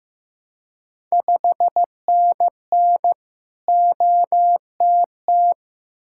Morse code, heard as a keyed tone, about 15 wpm, 700 Hz. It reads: 5NN OTT
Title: Ontario QSO Party Exchange 15wpm